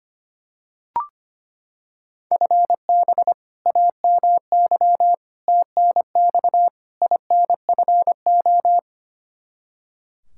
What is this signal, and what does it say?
Morse code, heard as a keyed tone, about 25 wpm, 700 Hz. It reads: FB AMY TNX INFO
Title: Practice QSO - Exchange 3 - More Info - 25wpm